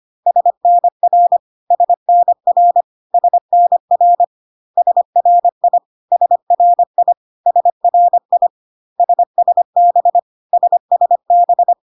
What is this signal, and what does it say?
Morse code, heard as a keyed tone, about 25 wpm, 700 Hz. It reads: SNR SNR SNR SRI SRI SRI SSB SSB
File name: CW abbreviations alphabetical repeated 3 times 25 25wpm